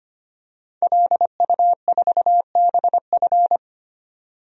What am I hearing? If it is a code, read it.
Morse code, heard as a keyed tone, about 25 wpm, 700 Hz. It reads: LU4BF
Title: DXCC-100 Call Signs - 5 Characters - 25wpm